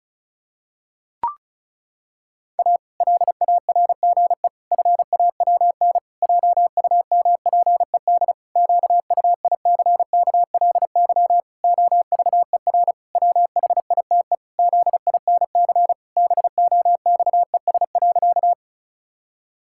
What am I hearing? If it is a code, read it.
Morse code, heard as a keyed tone, about 36 wpm, 700 Hz. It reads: A LARGE FAWN JUMPED QUICKLY OVER WHITE ZINC BOXES.
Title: Pangram Challenge v2 - 35wpm